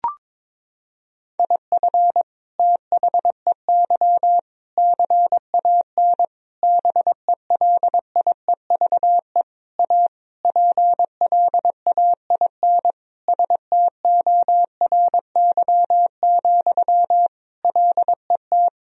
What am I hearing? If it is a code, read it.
Morse code, heard as a keyed tone, about 22 wpm, 700 Hz. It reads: IF THEY CAN BELIEVE A PLAIN STORY, LET
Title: Sentences from Top 1000 Words - Review Part 2 - 22wpm